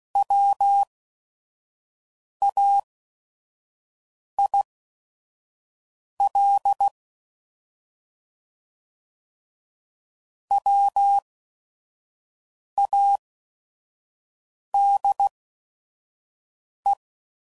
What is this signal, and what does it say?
Morse code, heard as a keyed tone, about 16 wpm, spaced as for 5 wpm, 800 Hz. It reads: WAIL WADE
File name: Prefixes / By First Letter / Ws